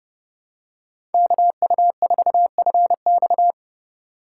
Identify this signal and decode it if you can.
Morse code, heard as a keyed tone, about 29 wpm, 700 Hz. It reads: KU4FX
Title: USA Call Signs - 4x Character Spacing - 30wpm